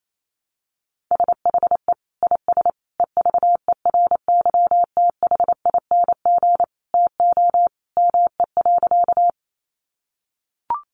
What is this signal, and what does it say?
Morse code, heard as a keyed tone, about 28 wpm, 700 Hz. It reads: SHE IS EVERYTHING TO ME.
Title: Sentences from Top 700 Words 28wpm